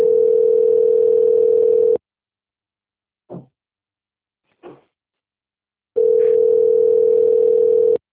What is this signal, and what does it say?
Ringback tone (North America), 440 + 480 Hz.